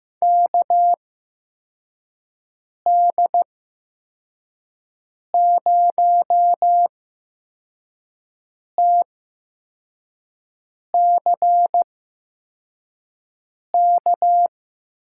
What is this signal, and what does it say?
Morse code, heard as a keyed tone, about 15 wpm, 700 Hz. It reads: K D 0 T C K
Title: USA Call Signs - 6 Characters - 8x Character Spacing - 15wpm